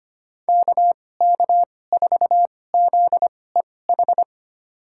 Morse code, heard as a keyed tone, about 25 wpm, spaced as for 18 wpm, 700 Hz. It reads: KK4ZEH
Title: USA Call Signs - 6 Characters - 2x Character Spacing - 25wpm